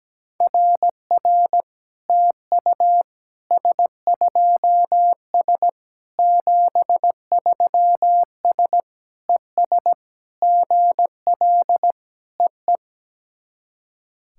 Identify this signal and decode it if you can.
Morse code, heard as a keyed tone, about 17 wpm, 700 Hz. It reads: RR TU S2S 73S ES GL EE
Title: SOTA QSO Practice 17wpm